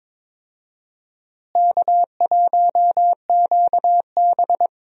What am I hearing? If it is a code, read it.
Morse code, heard as a keyed tone, about 22 wpm, 700 Hz. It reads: K1QB
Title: USA Call Signs - 4 Characters - Repeated 2x - 22wpm